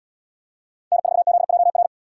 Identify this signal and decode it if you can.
Morse code, heard as a keyed tone, about 77 wpm, 700 Hz. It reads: N4ZJR